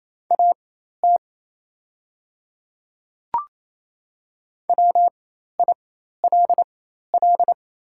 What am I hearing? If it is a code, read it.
Morse code, heard as a keyed tone, about 28 wpm, spaced as for 13 wpm, 690 Hz. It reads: AT WILL
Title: 4x Character Spacing - Top 100 Words - 28wpm